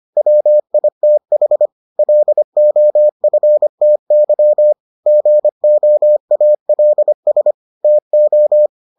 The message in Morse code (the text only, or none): WITH LOFTY GOALS TO